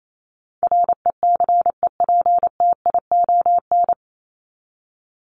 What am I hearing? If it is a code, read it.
Morse code, heard as a keyed tone, about 28 wpm, 700 Hz. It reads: RECEPTION
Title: Top 1751-2000 Words 28wpm